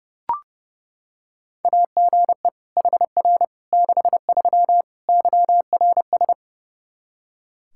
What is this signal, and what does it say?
Morse code, heard as a keyed tone, about 32 wpm, 700 Hz. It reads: AGE HR 63 YRS
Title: Practice QSO - Exchange 3 - More Info - 30wpm